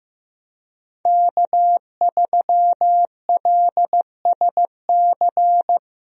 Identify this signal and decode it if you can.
Morse code, heard as a keyed tone, about 15 wpm, 700 Hz. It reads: K3LSC
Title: USA Call Signs - 5 Characters - Repeated 3x - 15wpm